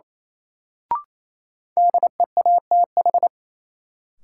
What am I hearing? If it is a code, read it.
Morse code, heard as a keyed tone, about 28 wpm, 700 Hz. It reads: DEATH
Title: Sentences from Top 1100 Words - New Word Drill - 28wpm